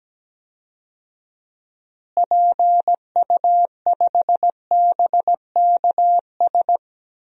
Morse code, heard as a keyed tone, about 17 wpm, 700 Hz. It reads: PU5BKS
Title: DXCC-100 Call Signs - 6 Characters - Repeated 3x - 17wpm